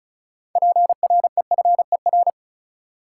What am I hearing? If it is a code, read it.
Morse code, heard as a keyed tone, about 34 wpm, 700 Hz. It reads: PREFER
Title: Six Letter Words - 35wpm